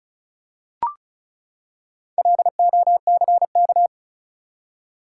Morse code, heard as a keyed tone, about 35 wpm, 700 Hz. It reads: LOCK
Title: Four Letter Words - 35wpm